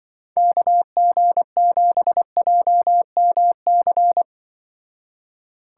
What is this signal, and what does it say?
Morse code, heard as a keyed tone, about 24 wpm, 700 Hz. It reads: KG7JMC
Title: Speed Racing - USA 2x3 Call Signs 20wpm